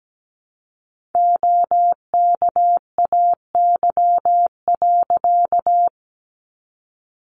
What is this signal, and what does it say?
Morse code, heard as a keyed tone, about 17 wpm, 700 Hz. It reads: OKAY.
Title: A Sound Of Thunder 17wpm